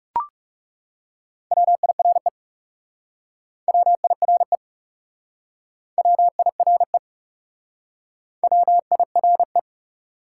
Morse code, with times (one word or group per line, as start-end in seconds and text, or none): 1.5-2.3 WIRE
3.7-4.6 WIRE
6.0-7.0 WIRE
8.4-9.6 WIRE